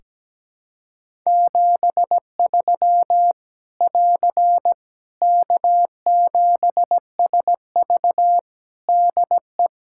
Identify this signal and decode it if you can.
Morse code, heard as a keyed tone, about 17 wpm, 700 Hz. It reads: 73 <AR> K7SV DE
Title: Practice QSO - Exchange 4 - Ending - 17wpm